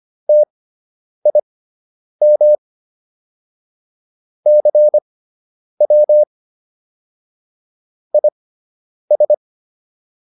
Morse code, heard as a keyed tone, about 25 wpm, spaced as for 9 wpm, 600 Hz. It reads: TIM CW IS